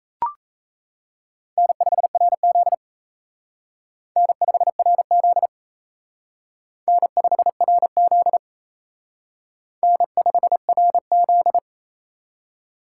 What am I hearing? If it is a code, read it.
Morse code, heard as a keyed tone, about 35 wpm, 700 Hz. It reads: N5RZ N5RZ N5RZ N5RZ
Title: Speed Racing - USA 1x2 Call Signs 28wpm